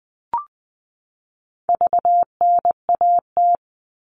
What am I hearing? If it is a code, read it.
Morse code, heard as a keyed tone, about 20 wpm, 700 Hz. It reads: VNAT
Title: All Cut Numbers - 4-Digit 20wpm